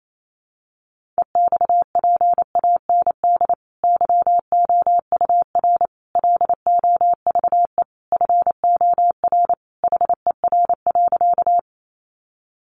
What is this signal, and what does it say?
Morse code, heard as a keyed tone, about 28 wpm, 700 Hz. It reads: EXPAND YOUR LOVE FOR HER.
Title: Sentences from Top 1400 Words 28wpm